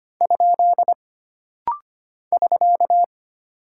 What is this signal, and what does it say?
Morse code, heard as a keyed tone, about 25 wpm, 700 Hz. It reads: ? <SK>
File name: Single Punctuation-Prosign - Rapid-Fire - 25wpm